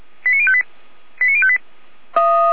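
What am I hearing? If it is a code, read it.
Five-tone selective call (ZVEI), heard as quick runs of five tones, tones of about 70 ms. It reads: 89058 89058
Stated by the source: ZVEI five-tone selective call received on an SDR.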